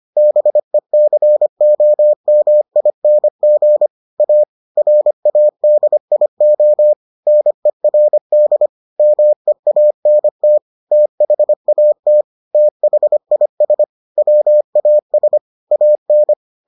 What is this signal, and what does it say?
Morse code, heard as a keyed tone, about 25 wpm, 600 Hz. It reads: BECOMING A RADIO NERD MEANT THAT THIS WAS AN